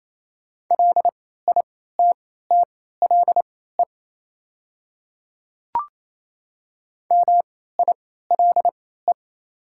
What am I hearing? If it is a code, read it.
Morse code, heard as a keyed tone, about 27 wpm, spaced as for 16 wpm, 700 Hz. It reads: LITTLE MILE
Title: 3x Character Spacing - Top 300 Words - 28wpm